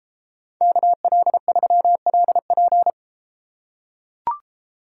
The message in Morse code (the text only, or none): KL3LP